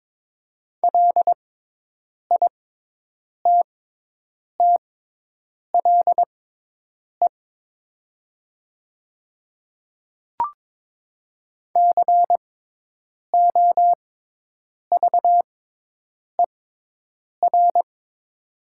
Morse code, heard as a keyed tone, about 22 wpm, spaced as for 8 wpm, 700 Hz. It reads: LITTLE COVER
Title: 6x Character Spacing - Top 300 Words - 22wpm